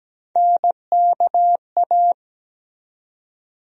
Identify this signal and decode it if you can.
Morse code, heard as a keyed tone, about 17 wpm, 700 Hz. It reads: NKA
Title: USA Call Sign Postfixes - 3 Letters 17wpm